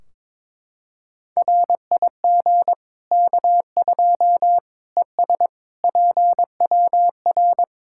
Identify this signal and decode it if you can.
Morse code, heard as a keyed tone, about 22 wpm, 700 Hz. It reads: RIG K2 ES PWR
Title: Practice QSO - Exchange 2 - Info - 22wpm